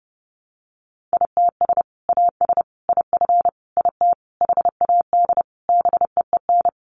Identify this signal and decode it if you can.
Morse code, heard as a keyed tone, about 30 wpm, 700 Hz. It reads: ITS AS IF IT HAD BEEN
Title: Sentences from Top 800 Words 30wpm